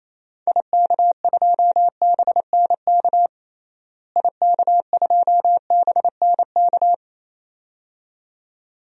Morse code, heard as a keyed tone, about 28 wpm, 700 Hz. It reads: IK2BNK IK2BNK
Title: DXCC-100 Call Signs - 6 Characters - Repeated 2x - 28wpm